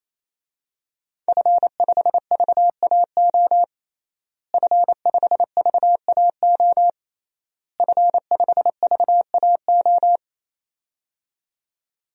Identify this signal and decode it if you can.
Morse code, heard as a keyed tone, about 28 wpm, 700 Hz. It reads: F5VAO F5VAO F5VAO
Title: DXCC-100 Call Signs - 5 Characters - Repeated 3x - 28wpm